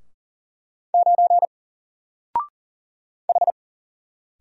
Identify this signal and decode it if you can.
Morse code, heard as a keyed tone, about 38 wpm, 700 Hz. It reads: 9 H
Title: Single Character (Letter:Number:Punctuation:Prosign) 40wpm